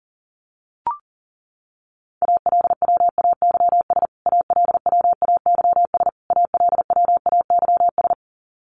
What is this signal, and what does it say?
Morse code, heard as a keyed tone, about 40 wpm, 700 Hz. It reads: ALWAYS ALWAYS ALWAYS
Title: Top 500 Words Repeated 40wpm